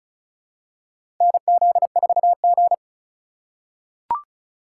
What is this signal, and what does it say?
Morse code, heard as a keyed tone, about 36 wpm, 700 Hz. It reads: NZ4G
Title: USA Call Signs - 2x Character Spacing - 35wpm